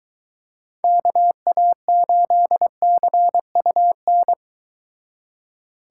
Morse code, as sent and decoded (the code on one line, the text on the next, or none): -.- .- ---.. -.-. ..- -.
KA8CUN